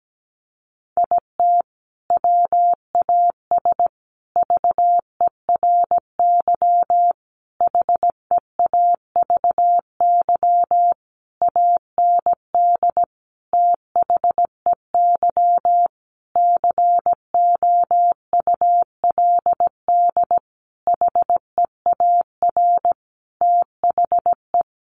Morse code, heard as a keyed tone, about 17 wpm, 700 Hz. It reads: IT WAS VERY HEAVY AND THEY COULD HEAR THE
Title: Sentences from Top 1100 Words 17wpm